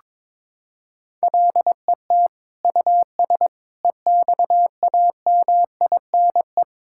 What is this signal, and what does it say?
Morse code, heard as a keyed tone, about 21 wpm, 700 Hz. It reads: LET US EXAMINE
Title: Sentences from Top 1600 Words 22wpm